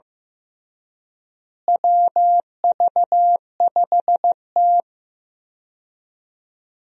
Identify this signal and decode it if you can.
Morse code, heard as a keyed tone, about 16 wpm, 700 Hz. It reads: WV5T
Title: USA Call Signs - 4 Characters - Repeated 3x - 15wpm